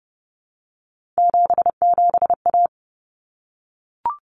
Morse code, heard as a keyed tone, about 30 wpm, 700 Hz. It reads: 77A
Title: Common Cut Numbers - 3-Digit 30wpm